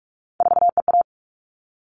Morse code, heard as a keyed tone, about 44 wpm, 700 Hz. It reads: 4EA
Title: All Cut Numbers - 3-Digit 45wpm